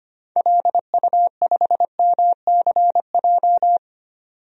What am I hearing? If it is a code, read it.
Morse code, heard as a keyed tone, about 25 wpm, 700 Hz. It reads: LU5MCJ
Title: DXCC-100 Call Signs - 6 Characters - 25wpm